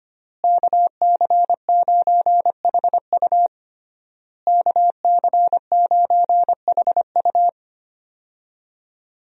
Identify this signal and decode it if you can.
Morse code, heard as a keyed tone, about 25 wpm, 700 Hz. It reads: KC9HU KC9HU
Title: USA Call Signs - 5 Characters - Repeated 3x - 25wpm